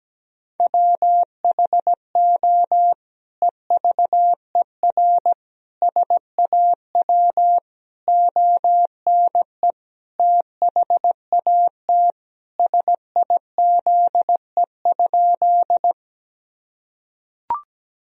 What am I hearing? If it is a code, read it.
Morse code, heard as a keyed tone, about 17 wpm, 700 Hz. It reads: WHO EVER SAW ONE THAT SIZE?
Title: Sentences from Top 1000 Words - Review Part 2 - 17wpm